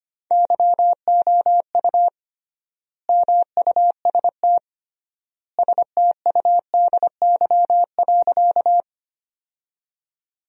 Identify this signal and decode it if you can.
Morse code, heard as a keyed tone, about 25 wpm, 700 Hz. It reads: YOU MUST STUDY.
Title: 3x Word-Spacing - Sentences from Top 300 Words - 25wpm